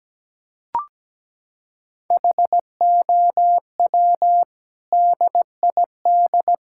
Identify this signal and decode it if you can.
Morse code, heard as a keyed tone, about 17 wpm, 700 Hz. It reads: HOW DID
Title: Sentences from Top 1000 Words - Review Part 5 - 17wpm